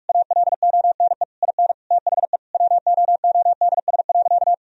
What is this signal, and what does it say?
Morse code, heard as a keyed tone, about 46 wpm, 700 Hz. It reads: ALONE IN THE WOODS.
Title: Sentences from Top 1000 Words - Review Part 2 - 45wpm